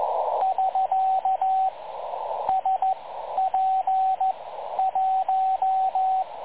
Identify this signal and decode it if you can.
Morse code, heard as a keyed tone, about 15 wpm, spaced as for 11 wpm, 750 Hz, with radio noise. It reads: <SK> SP1
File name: dlazaawansowanych / nagraniazpasma / websdr recording 3565.0kHz